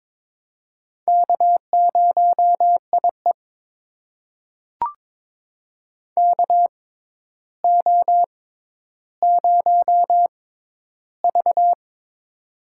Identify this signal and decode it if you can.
Morse code, heard as a keyed tone, about 21 wpm, spaced as for 11 wpm, 700 Hz. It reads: K0IE KO0V
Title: USA Call Signs - 4 Characters - 6x Character Spacing - 22wpm